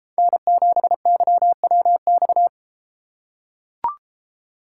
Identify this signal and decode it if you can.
Morse code, heard as a keyed tone, about 33 wpm, 700 Hz. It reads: N7YWX